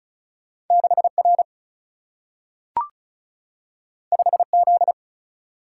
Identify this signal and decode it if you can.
Morse code, heard as a keyed tone, about 36 wpm, 700 Hz. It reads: 6 R 5 Z
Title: Single-Letter Single-Number 35wpm